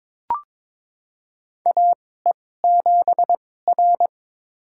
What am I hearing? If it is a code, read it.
Morse code, heard as a keyed tone, about 22 wpm, spaced as for 16 wpm, 700 Hz. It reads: AE7R
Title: USA Call Signs - 4 Characters - 2x Character Spacing - 22wpm